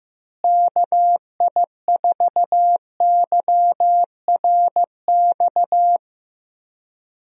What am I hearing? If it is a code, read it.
Morse code, heard as a keyed tone, about 15 wpm, 700 Hz. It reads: KI4YRX
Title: USA Call Signs - 6 Characters - Repeated 2x - 15wpm